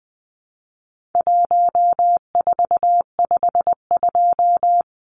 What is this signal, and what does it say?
Morse code, heard as a keyed tone, about 20 wpm, 700 Hz. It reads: 1452
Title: Numbers - Contest Style - 4-Digit 20wpm